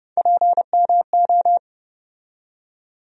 Morse code, heard as a keyed tone, about 31 wpm, 700 Hz. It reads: PMO
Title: USA Call Sign Postfixes - 3 Letters 30wpm